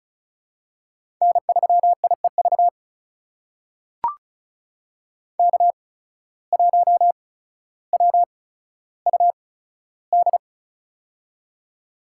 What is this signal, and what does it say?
Morse code, heard as a keyed tone, about 36 wpm, spaced as for 15 wpm, 700 Hz. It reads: N3IEV K1WUD
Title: USA Call Signs - 8x Character Spacing - 35wpm